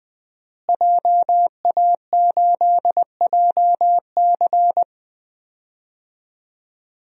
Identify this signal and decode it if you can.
Morse code, heard as a keyed tone, about 20 wpm, 700 Hz. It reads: JA8JC
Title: DXCC-100 Call Signs - 5 Characters - Repeated 2x - 20wpm